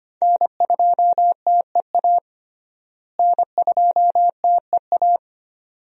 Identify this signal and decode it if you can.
Morse code, heard as a keyed tone, about 25 wpm, 700 Hz. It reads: N2TEA N2TEA
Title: USA Call Signs - 5 Characters - Repeated 2x - 25wpm